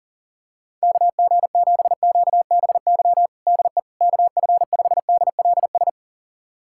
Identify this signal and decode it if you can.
Morse code, heard as a keyed tone, about 40 wpm, 700 Hz. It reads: KG7QBY DE KF5DLS
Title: USA Callsign DE Callsign - 6 Characters - 40wpm